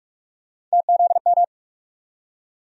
Morse code, heard as a keyed tone, about 47 wpm, 700 Hz. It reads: TZM
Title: USA Call Sign Postfixes - 3 Letters 45wpm